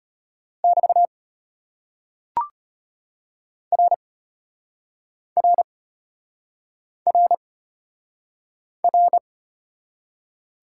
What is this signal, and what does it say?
Morse code, heard as a keyed tone, about 35 wpm, 700 Hz. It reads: <BT> R R R R